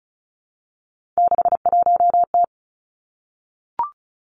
Morse code, heard as a keyed tone, about 36 wpm, 700 Hz. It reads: 61T
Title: Three Challenging Letter-Number Combinations 35wpm